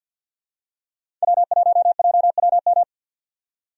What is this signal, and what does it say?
Morse code, heard as a keyed tone, about 53 wpm, 700 Hz. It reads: W1JWM
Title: USA Call Signs - 5 Characters - 50wpm